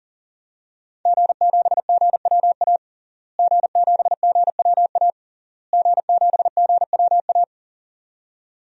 Morse code, heard as a keyed tone, about 40 wpm, 700 Hz. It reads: G7GWA G7GWA G7GWA
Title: DXCC-100 Call Signs - 5 Characters - Repeated 3x - 40wpm